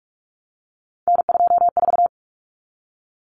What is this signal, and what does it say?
Morse code, heard as a keyed tone, about 44 wpm, 700 Hz. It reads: N24
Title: Common Cut Numbers - 3-Digit 45wpm